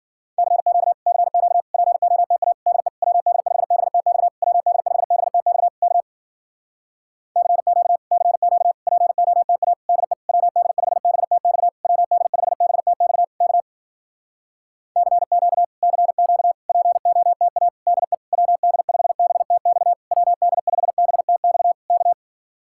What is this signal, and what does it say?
Morse code, heard as a keyed tone, about 52 wpm, 700 Hz. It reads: CQ CQ POTA DE WD5BTX WD5BTX K CQ CQ POTA DE WD5BTX WD5BTX K CQ CQ POTA DE WD5BTX WD5BTX K